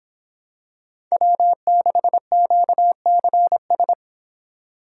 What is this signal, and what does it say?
Morse code, heard as a keyed tone, about 26 wpm, 700 Hz. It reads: W6QCS